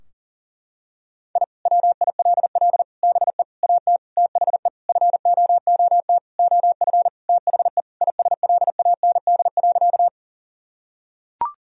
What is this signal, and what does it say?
Morse code, heard as a keyed tone, about 40 wpm, 700 Hz. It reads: I WILL BE AT THE FOOT OF THE ISLAND.
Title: Sentences from Top 1000 Words - Review Part 3 - 40wpm